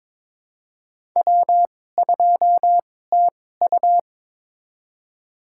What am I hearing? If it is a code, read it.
Morse code, heard as a keyed tone, about 22 wpm, spaced as for 16 wpm, 700 Hz. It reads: W2TU
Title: USA Call Signs - 4 Characters - 2x Character Spacing - 22wpm